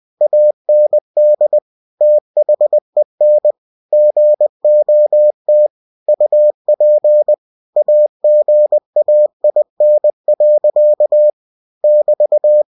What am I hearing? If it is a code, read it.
Morse code, heard as a keyed tone, about 20 wpm, 600 Hz. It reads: AND THEN GOT UP AGAIN. <BT>
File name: cw-books / winnie-the-pooh / winnie-20wpm-page-0009